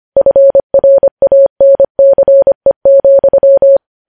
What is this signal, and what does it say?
Morse code, heard as a keyed tone, about 25 wpm, 550 Hz. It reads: FRANCE,
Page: news-headlines-in-morse-code-at-25-wpm